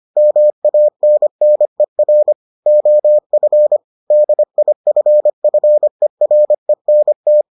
Morse code, heard as a keyed tone, about 25 wpm, 600 Hz. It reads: MANNER OF DIFFERENT